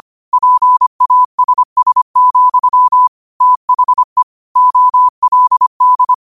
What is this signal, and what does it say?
Morse code, heard as a keyed tone, about 25 wpm, 1000 Hz. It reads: PASS, THE OLD
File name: qotdfiles / 25wpm